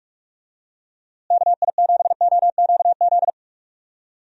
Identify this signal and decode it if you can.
Morse code, heard as a keyed tone, about 44 wpm, 700 Hz. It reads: KI7OQZ
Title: USA Call Signs - 6 Characters - Repeated 3x - 45wpm